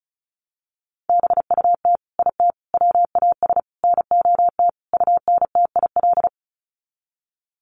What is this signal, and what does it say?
Morse code, heard as a keyed tone, about 34 wpm, 700 Hz. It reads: BUT IT WAS NOT UNTIL
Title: Sets of 5 Words Encore2 35wpm